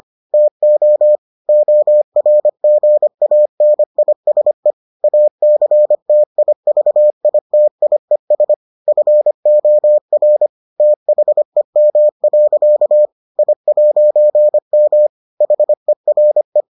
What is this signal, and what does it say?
Morse code, heard as a keyed tone, about 25 wpm, 600 Hz. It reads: TO ORGANISE ACTIVITIES FOR THEM. I'M HERE